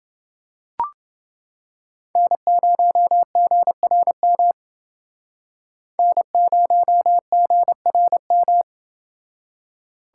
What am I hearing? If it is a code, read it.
Morse code, heard as a keyed tone, about 29 wpm, 700 Hz. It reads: N0GRM N0GRM